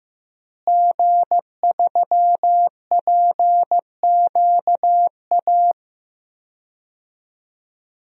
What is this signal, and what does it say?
Morse code, heard as a keyed tone, about 16 wpm, 700 Hz. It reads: G3PQA
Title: DXCC-100 Call Signs - 5 Characters - Repeated 2x - 15wpm